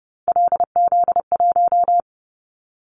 Morse code, heard as a keyed tone, about 30 wpm, 700 Hz. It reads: LZ1
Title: Callsign Prefixes for Top 50 DXCC - 30wpm